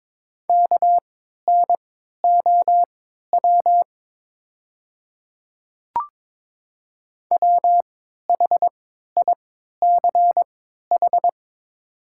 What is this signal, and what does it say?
Morse code, heard as a keyed tone, about 23 wpm, spaced as for 13 wpm, 700 Hz. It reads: KNOW WHICH
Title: 3x Character Spacing - Top 100 Words - 22wpm